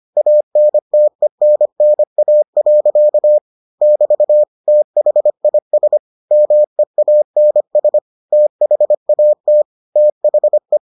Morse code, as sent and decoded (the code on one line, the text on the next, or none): .- -. - . -. -. .- .-.-.- / -...- / - .... .. ... / -- . .- -. ... / - .... .- - / - .... .
ANTENNA. <BT> THIS MEANS THAT THE